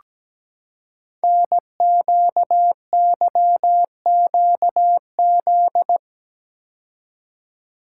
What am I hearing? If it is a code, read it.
Morse code, heard as a keyed tone, about 17 wpm, 700 Hz. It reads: NQYQZ
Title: Code Groups - 5 Letters - 17wpm